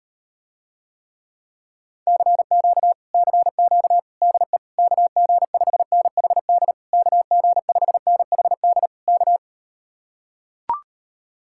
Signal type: Morse code

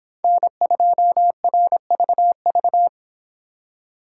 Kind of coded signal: Morse code

